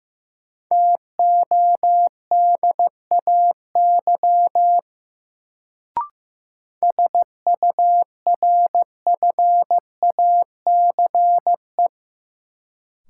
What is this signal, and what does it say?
Morse code, heard as a keyed tone, about 15 wpm, 700 Hz. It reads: TODAY SURFACE
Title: Sentences from Top 600 Words - New Word Drill - 15wpm